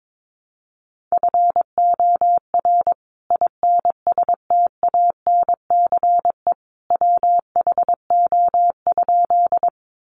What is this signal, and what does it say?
Morse code, heard as a keyed tone, about 22 wpm, 700 Hz. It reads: FOR INSTANCE WHO?
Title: Sentences from Top 800 Words 22wpm